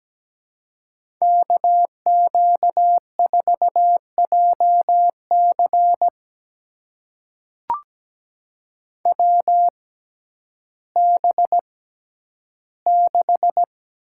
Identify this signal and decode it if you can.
Morse code, heard as a keyed tone, about 17 wpm, spaced as for 10 wpm, 700 Hz. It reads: KQ4JC WB6
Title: USA Call Signs - 6x Character Spacing - 17wpm